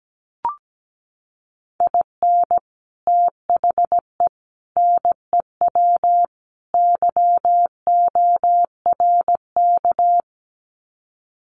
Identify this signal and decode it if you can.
Morse code, heard as a keyed tone, about 17 wpm, 700 Hz. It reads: IN THE NEW YORK
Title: Sets of 4 Words 17wpm